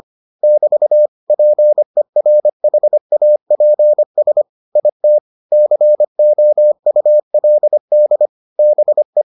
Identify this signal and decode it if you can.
Morse code, heard as a keyed tone, about 25 wpm, 600 Hz. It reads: <BT> PERHAPS IT COULD BE